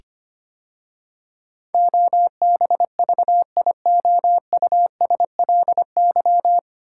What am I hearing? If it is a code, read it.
Morse code, heard as a keyed tone, about 25 wpm, 700 Hz. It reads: OBVIOUSLY
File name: practice / 25wpm / 9 Letter Words - First 3 Letters Given - 25wpm